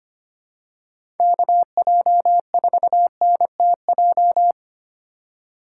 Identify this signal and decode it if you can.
Morse code, heard as a keyed tone, about 25 wpm, 700 Hz. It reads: KJ4NTJ
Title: USA Call Signs - 6 Characters - 2x Character Spacing - 25wpm